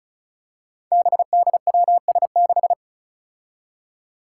Morse code, heard as a keyed tone, about 36 wpm, 700 Hz. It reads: BDWS6
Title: Five Challenging Letter-Number Combinations 35wpm